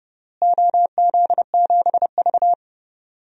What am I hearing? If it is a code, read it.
Morse code, heard as a keyed tone, about 30 wpm, 700 Hz. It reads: OZ7V
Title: DXCC-100 Call Signs - 4 Characters - Repeated 3x - 30wpm